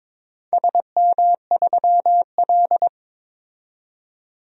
Morse code, heard as a keyed tone, about 22 wpm, 700 Hz. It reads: SM3L
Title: DXCC-100 Call Signs - 4 Characters - 22wpm